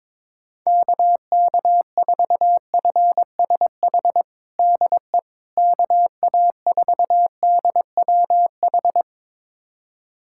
Morse code, heard as a keyed tone, about 22 wpm, 700 Hz. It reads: KK4FSH DE KA4DWH
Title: USA Callsign DE Callsign - 6 Characters - 22wpm